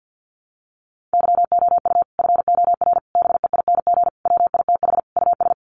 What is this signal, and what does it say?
Morse code, heard as a keyed tone, about 50 wpm, 700 Hz. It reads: YOU FOR BEING WITH US
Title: Sets of 5 Words 50wpm